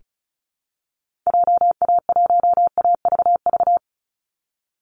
Morse code, heard as a keyed tone, about 34 wpm, 700 Hz. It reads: JA1AVV
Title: USA Call Signs 35wpm